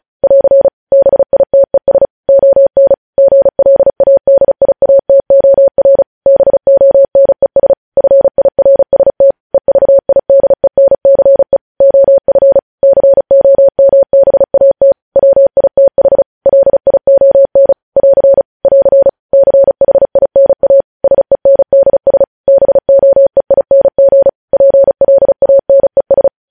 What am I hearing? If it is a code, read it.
Morse code, heard as a keyed tone, about 36 wpm, 550 Hz. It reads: <AR> BITES ON GLADIATOR BONES FIRST EVIDENCE OF COMBAT WITH LION <AR> <AR> CHINA SENDS BOEING PLANES
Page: news headlines in morse code at 35 wpm